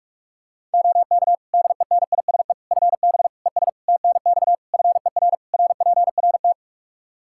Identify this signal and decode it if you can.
Morse code, heard as a keyed tone, about 46 wpm, 700 Hz. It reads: OK DENISE FB ES TNX FER RPRT
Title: K1USN SST QSO Practice 45wpm